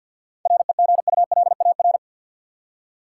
Morse code, heard as a keyed tone, about 52 wpm, 700 Hz. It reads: REGULAR
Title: Seven Letter Words - 50wpm